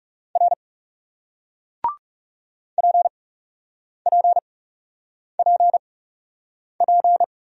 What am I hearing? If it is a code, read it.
Morse code, heard as a keyed tone, about 37 wpm, spaced as for 8 wpm, 700 Hz. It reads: R PPPP
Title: Speed-Racing Single Letters 30wpm